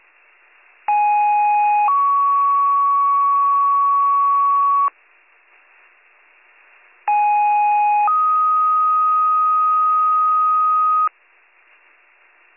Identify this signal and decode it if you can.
Two-tone page, heard as a one-second tone, then a three-second tone, about 830 then 1120 Hz, about 830 then 1250 Hz. It is heard from 0.9 to 11.1 s.